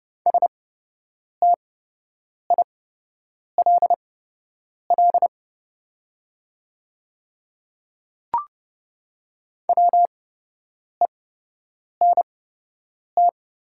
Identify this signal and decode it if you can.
Morse code, heard as a keyed tone, about 30 wpm, spaced as for 8 wpm, 700 Hz. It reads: STILL WENT
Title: 8x Character Spacing - Top 500 Words - 30wpm